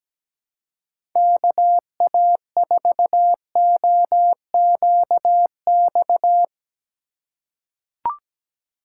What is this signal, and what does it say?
Morse code, heard as a keyed tone, about 17 wpm, 700 Hz. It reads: KA4OQX
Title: USA Call Signs - 6 Characters - 3x Character Spacing - 17wpm